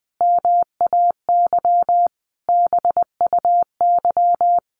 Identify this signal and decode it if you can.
Morse code, heard as a keyed tone, about 20 wpm, 700 Hz. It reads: MAY BUY
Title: Sentences from Top 800 Words 20wpm